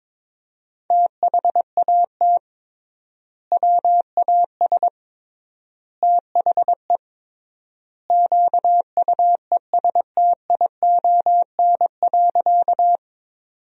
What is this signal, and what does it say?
Morse code, heard as a keyed tone, about 22 wpm, 700 Hz. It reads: THAT WAS THE QUESTION.
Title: 3x Word-Spacing - Sentences from Top 400 Words - 22wpm